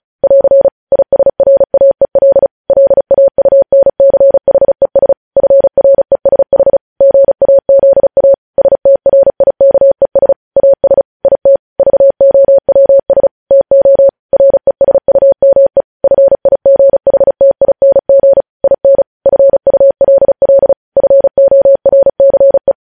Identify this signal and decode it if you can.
Morse code, heard as a keyed tone, about 36 wpm, 550 Hz. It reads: <AR> ISRAEL LAUNCHES FRESH GAZA STRIKES AS IT VOWS TO RESUME FIGHTING IN FULL FORCE
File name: podcast / News headlines-1x-35x35